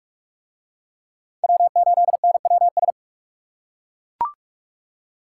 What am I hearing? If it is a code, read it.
Morse code, heard as a keyed tone, about 42 wpm, 700 Hz. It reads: W8NWS